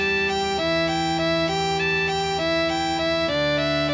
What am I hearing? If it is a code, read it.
There is no code.